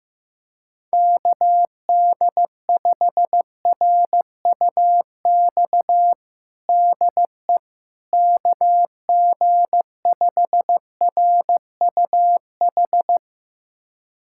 Morse code, heard as a keyed tone, about 15 wpm, 700 Hz. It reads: KD5RUX DE KG5RUH
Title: USA Callsign DE Callsign - 6 Characters - 15wpm